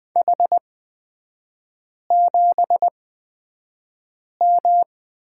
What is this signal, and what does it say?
Morse code, heard as a keyed tone, about 20 wpm, spaced as for 5 wpm, 700 Hz. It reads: H7M
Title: 236 Lesson 38e - Letter q - Three Characters 20wpm